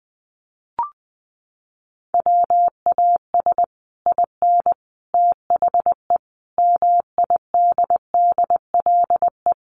Morse code, heard as a keyed tone, about 20 wpm, 700 Hz. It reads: WAS IN THE MIDDLE
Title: Sets of 5 Words Encore 20wpm